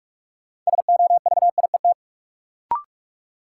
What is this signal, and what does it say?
Morse code, heard as a keyed tone, about 42 wpm, 700 Hz. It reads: SOVIET
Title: Sets of 5 Words Encore - New Word Drill - 45wpm